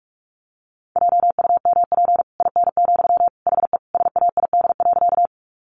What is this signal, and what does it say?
Morse code, heard as a keyed tone, about 44 wpm, 700 Hz. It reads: JUMP IN, HE SAID.
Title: Sentences from Top 1300 Words 45wpm